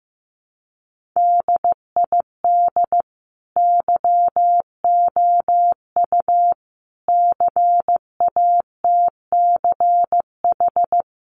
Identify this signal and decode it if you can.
Morse code, heard as a keyed tone, about 15 wpm, 700 Hz. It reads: DID YOU CATCH
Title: Sentences from Top 1500 Words 15wpm